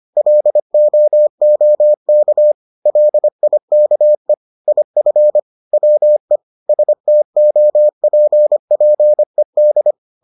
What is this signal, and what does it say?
Morse code, heard as a keyed tone, about 25 wpm, 600 Hz. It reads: LOOK LIKE IF WE STOPPED